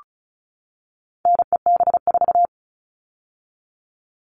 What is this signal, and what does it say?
Morse code, heard as a keyed tone, about 34 wpm, 700 Hz. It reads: NEB4
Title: All Cut Numbers - 4-Digit 35wpm